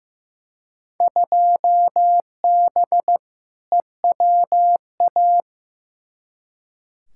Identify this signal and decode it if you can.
Morse code, heard as a keyed tone, about 15 wpm, 700 Hz. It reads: 2B EWA
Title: Field Day Exchanges Encore - 15wpm